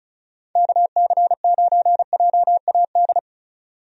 Morse code, heard as a keyed tone, about 34 wpm, 700 Hz. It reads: KC9JAD
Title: USA Call Signs - 6 Characters - 35wpm